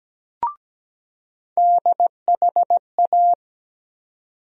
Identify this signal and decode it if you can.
Morse code, heard as a keyed tone, about 17 wpm, 700 Hz. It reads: DHA
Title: USA Call Sign Postfixes - 3 Letters 17wpm